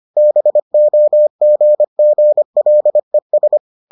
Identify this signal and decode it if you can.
Morse code, heard as a keyed tone, about 25 wpm, 600 Hz. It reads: BOGGLES